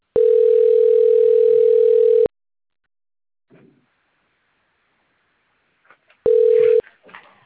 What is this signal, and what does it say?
Ringback tone (North America), 440 + 480 Hz.